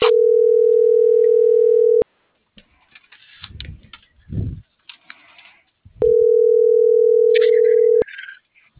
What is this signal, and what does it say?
Ringback tone (North America), 440 + 480 Hz.